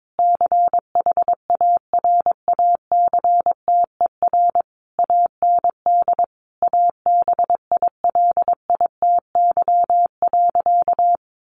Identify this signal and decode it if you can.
Morse code, heard as a keyed tone, about 22 wpm, 700 Hz. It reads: CHARACTER AND ABILITY.
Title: Sentences from Top 1200 Words 22wpm